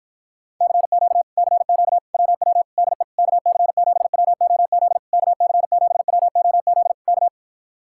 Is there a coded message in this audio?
Yes: Morse code from 0.6 to 7.3 s, heard as a keyed tone, about 56 wpm, 700 Hz. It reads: CQ CQ WW DE KC7WOZ KC7WOZ K